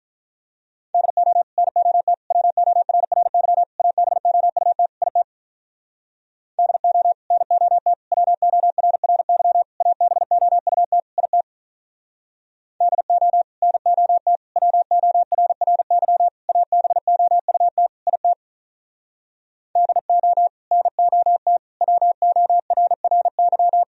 Morse code, heard as a keyed tone, about 46 wpm, 700 Hz. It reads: DO NOT WORRY ABOUT IT DO NOT WORRY ABOUT IT DO NOT WORRY ABOUT IT DO NOT WORRY